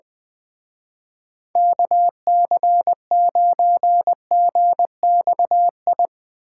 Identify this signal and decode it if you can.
Morse code, heard as a keyed tone, about 20 wpm, 700 Hz. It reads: KC9GXI
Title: Speed Racing - USA Call Signs 20wpm